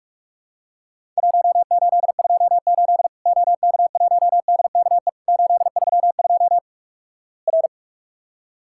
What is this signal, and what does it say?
Morse code, heard as a keyed tone, about 48 wpm, 700 Hz. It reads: 1828 OK1DKE 832 R
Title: Worked All Europe DX Contest - QTC Exchange - 45wpm